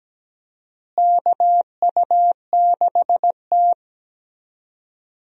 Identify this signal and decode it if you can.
Morse code, heard as a keyed tone, about 17 wpm, 700 Hz. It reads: KU6T
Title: USA Call Signs - 4 Characters - Repeated 2x - 17wpm